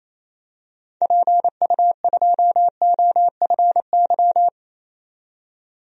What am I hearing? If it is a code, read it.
Morse code, heard as a keyed tone, about 28 wpm, 700 Hz. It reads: PU2OFY
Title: DXCC-100 Call Signs - 6 Characters - 28wpm